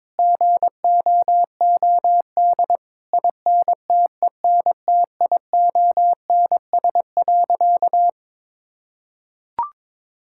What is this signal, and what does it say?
Morse code, heard as a keyed tone, about 22 wpm, 700 Hz. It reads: GOOD INTENTIONS.
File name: practice / 22wpm / Sentences with 8-Word Phrases - 22wpm